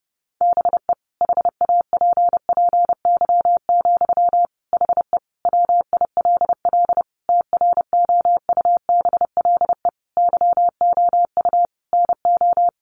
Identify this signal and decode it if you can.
Morse code, heard as a keyed tone, about 30 wpm, 700 Hz. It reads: BE HAPPY, HE WILL TROUBLE YOU NO
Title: Sentences from Top 1200 Words 30wpm